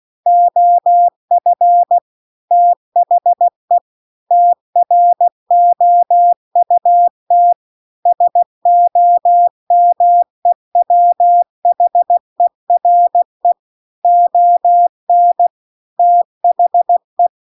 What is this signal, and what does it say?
Morse code, heard as a keyed tone, about 16 wpm, 700 Hz. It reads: OF THE TROUT SOMEWHERE ON THE
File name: CW-The-Big-Twoheaded-River-2-16WPM